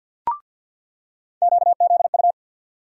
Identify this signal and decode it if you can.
Morse code, heard as a keyed tone, about 49 wpm, 700 Hz. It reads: QZU
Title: USA Call Sign Postfixes - 3 Letters 50wpm